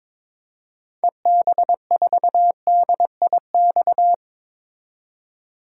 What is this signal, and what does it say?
Morse code, heard as a keyed tone, about 22 wpm, 700 Hz. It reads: EB4DIX
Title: DXCC-100 Call Signs - 6 Characters - 22wpm